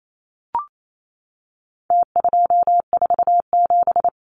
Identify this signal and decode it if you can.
Morse code, heard as a keyed tone, about 28 wpm, 700 Hz. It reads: T247
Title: Numbers - Contest Style - 4-Digit 28wpm